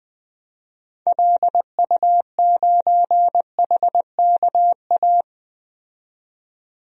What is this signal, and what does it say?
Morse code, heard as a keyed tone, about 20 wpm, 700 Hz. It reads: LU9HKA
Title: DXCC-100 Call Signs - 6 Characters - 20wpm